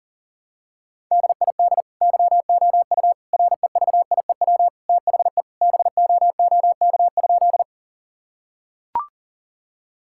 Morse code, heard as a keyed tone, about 40 wpm, 700 Hz. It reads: DID YOU REVIEW THE BOOK?
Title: Sentences from Top 1000 Words - Review Part 6 - 40wpm